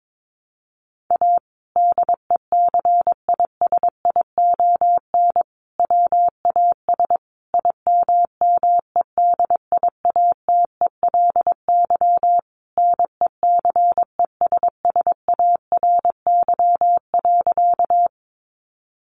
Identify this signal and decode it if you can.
Morse code, heard as a keyed tone, about 22 wpm, 700 Hz. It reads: A DECISION WAS IMMEDIATELY NECESSARY.
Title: Sentences from Top 1100 Words 22wpm